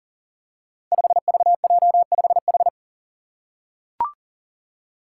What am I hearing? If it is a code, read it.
Morse code, heard as a keyed tone, about 40 wpm, 700 Hz. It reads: 5VJ5H
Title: Five Challenging Letter-Number Combinations 40wpm